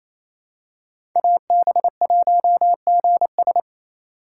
Morse code, heard as a keyed tone, about 28 wpm, 700 Hz. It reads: AB1GS
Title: USA Call Signs - 2x Character Spacing - 28wpm